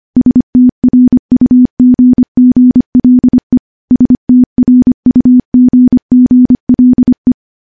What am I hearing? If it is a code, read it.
Morse code, heard as a keyed tone, about 25 wpm, 250 Hz. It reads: STRUGGLE STRUGGLE
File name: morse-training-25WPM